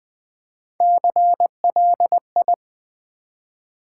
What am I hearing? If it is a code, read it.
Morse code, heard as a keyed tone, about 20 wpm, 700 Hz. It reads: CLI
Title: Build A Word - 3 Letters At A Time - 8-Letter Words - 20wpm